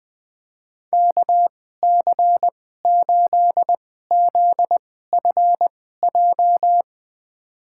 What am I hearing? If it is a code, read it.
Morse code, heard as a keyed tone, about 20 wpm, spaced as for 14 wpm, 700 Hz. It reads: KC8ZFJ
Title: USA Call Signs - 6 Characters - 2x Character Spacing - 20wpm